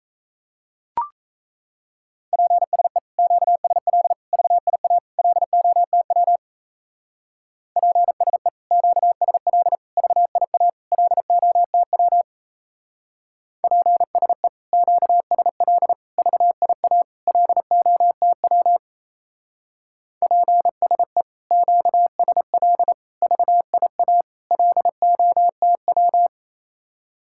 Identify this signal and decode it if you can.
Morse code, heard as a keyed tone, about 35 wpm, 700 Hz. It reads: PSE QSL VIA LOTW PSE QSL VIA LOTW PSE QSL VIA LOTW PSE QSL VIA LOTW
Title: Speed-Racing Common QSO Phrases Encore - 28wpm